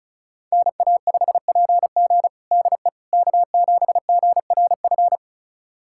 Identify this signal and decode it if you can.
Morse code, heard as a keyed tone, about 36 wpm, 700 Hz. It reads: NA5PG DE K7GRF
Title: USA Callsign DE Callsign - 5 Characters - 35wpm